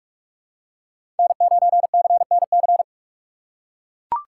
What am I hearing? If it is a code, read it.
Morse code, heard as a keyed tone, about 43 wpm, 700 Hz. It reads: N9CNC
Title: USA Call Signs - 5 Characters - 2x Character Spacing - 45wpm